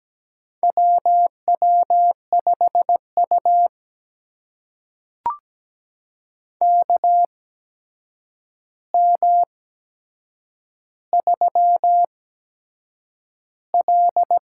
Morse code, heard as a keyed tone, about 17 wpm, spaced as for 7 wpm, 700 Hz. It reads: WW5U KM3L
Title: USA Call Signs - 4 Characters - 8x Character Spacing - 17wpm